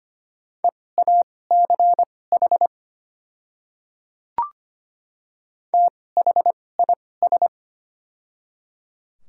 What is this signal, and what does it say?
Morse code, heard as a keyed tone, about 25 wpm, spaced as for 18 wpm, 690 Hz. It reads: EACH THIS
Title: 2x Character Spacing - Top 100 Words - 25wpm